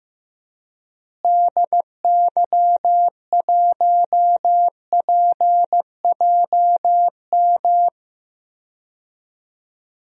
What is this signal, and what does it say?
Morse code, heard as a keyed tone, about 15 wpm, 700 Hz. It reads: DY1PJM
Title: DXCC-100 Call Signs - 6 Characters - Repeated 3x - 15wpm